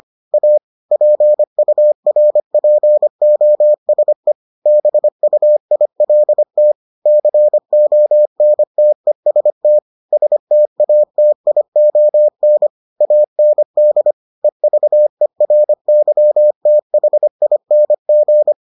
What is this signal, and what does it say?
Morse code, heard as a keyed tone, about 25 wpm, 600 Hz. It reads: A PURPOSE BUILT CONTEST STATION AND EVERYTHING